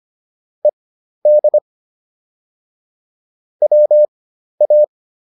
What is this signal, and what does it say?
Morse code, heard as a keyed tone, about 25 wpm, spaced as for 12 wpm, 600 Hz. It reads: ED WA
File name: ED6 12